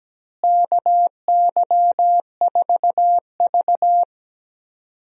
Morse code, heard as a keyed tone, about 17 wpm, 700 Hz. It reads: KY4V
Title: USA Call Signs - 4 Characters - Repeated 2x - 17wpm